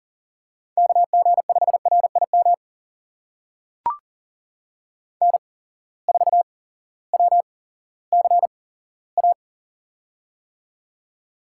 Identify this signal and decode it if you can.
Morse code, heard as a keyed tone, about 40 wpm, spaced as for 18 wpm, 700 Hz. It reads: KG5RIM N4WCA
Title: USA Call Signs - 8x Character Spacing - 40wpm